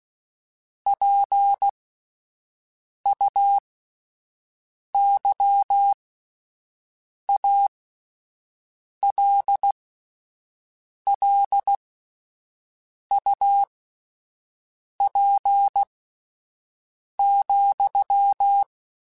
Morse code, heard as a keyed tone, about 16 wpm, spaced as for 5 wpm, 800 Hz. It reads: PUYALLUP,
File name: (spaced) / QSOs Spaced-mp3s / kc7haa @10 sp